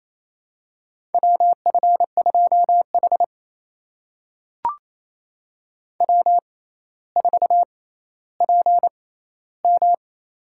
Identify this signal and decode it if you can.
Morse code, heard as a keyed tone, about 28 wpm, spaced as for 14 wpm, 700 Hz. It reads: WF2H W4PM
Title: USA Call Signs - 4 Characters - 6x Character Spacing - 28wpm